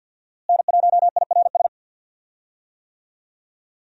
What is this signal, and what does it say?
Morse code, heard as a keyed tone, about 47 wpm, 700 Hz. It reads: N1IRS
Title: USA Call Signs - 5 Characters - Repeated 3x - 50wpm